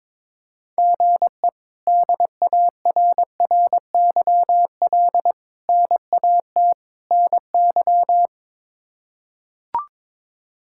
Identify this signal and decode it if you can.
Morse code, heard as a keyed tone, about 21 wpm, 700 Hz. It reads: GE DARRYL NAT NY
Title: K1USN SST QSO Practice - 22wpm